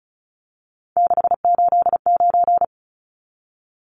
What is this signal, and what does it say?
Morse code, heard as a keyed tone, about 34 wpm, 700 Hz. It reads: 689
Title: Numbers - 3-Digit 35wpm